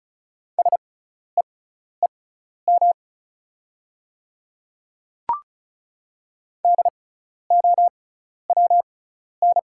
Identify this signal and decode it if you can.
Morse code, heard as a keyed tone, about 36 wpm, spaced as for 12 wpm, 690 Hz. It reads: SEEM DOWN
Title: 6x Character Spacing - Top 300 Words - 35wpm